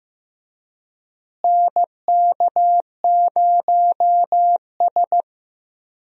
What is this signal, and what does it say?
Morse code, heard as a keyed tone, about 16 wpm, 700 Hz. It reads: NK0S
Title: USA Call Signs - 4 Characters - Repeated 3x - 15wpm